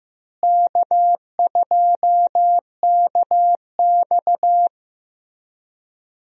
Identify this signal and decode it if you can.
Morse code, heard as a keyed tone, about 15 wpm, 700 Hz. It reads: K2KX
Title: USA Call Signs - 4 Characters - Repeated 3x - 15wpm